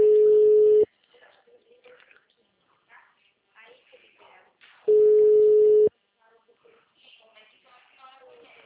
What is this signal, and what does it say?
Ringback tone (Europe, 425 Hz), with speech.